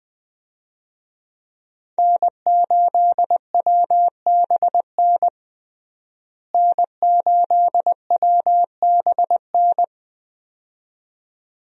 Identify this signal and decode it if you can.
Morse code, heard as a keyed tone, about 20 wpm, 700 Hz. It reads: N8WBN N8WBN
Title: USA Call Signs - 5 Characters - Repeated 2x - 20wpm